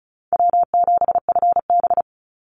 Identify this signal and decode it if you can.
Morse code, heard as a keyed tone, about 34 wpm, 700 Hz. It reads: W7FB
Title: Call Signs Repeated 3x - 35wpm